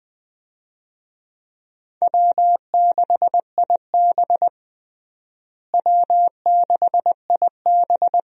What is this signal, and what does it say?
Morse code, heard as a keyed tone, about 21 wpm, 700 Hz. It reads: W6IB W6IB
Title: USA Call Signs - 4 Characters - Repeated 3x - 20wpm